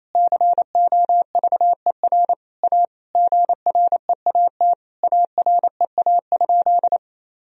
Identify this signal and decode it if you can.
Morse code, heard as a keyed tone, about 28 wpm, 700 Hz. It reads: COVER A GREAT AREA?
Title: Sentences from Top 1000 Words - Review Part 2 - 28wpm